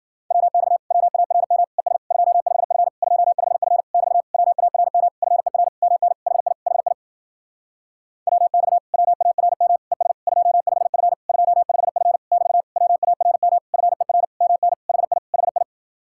Morse code, heard as a keyed tone, about 52 wpm, 700 Hz. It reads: WX WARM ES 25F 25F <BT> WARM FER MN HI HI WX WARM ES 25F 25F <BT> WARM FER MN HI HI